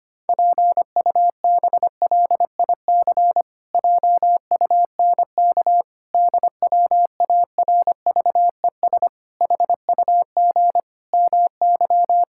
Morse code, heard as a keyed tone, about 25 wpm, 700 Hz. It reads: PUBLIC JUNK DWARVES HUG MY
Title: Pangram Challenge v2 - 25wpm